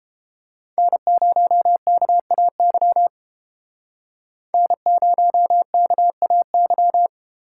Morse code, heard as a keyed tone, about 33 wpm, 700 Hz. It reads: N0KAY N0KAY